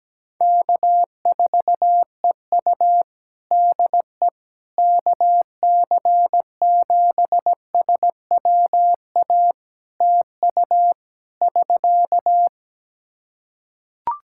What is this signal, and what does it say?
Morse code, heard as a keyed tone, about 17 wpm, 700 Hz. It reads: K4EU DE KC7SWA TU <SK>
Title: Practice QSO - Exchange 4 - Ending - 17wpm